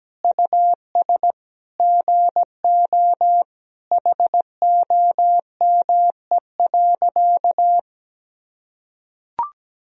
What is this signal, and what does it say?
Morse code, heard as a keyed tone, about 17 wpm, 700 Hz. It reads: US GO HOME.
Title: Sentences from Top 300 Words and Abbreviations 17wpm